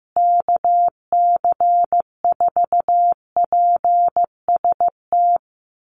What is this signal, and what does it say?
Morse code, heard as a keyed tone, about 15 wpm, 700 Hz. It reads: KC4PST
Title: USA Call Signs Repeated 3x 15wpm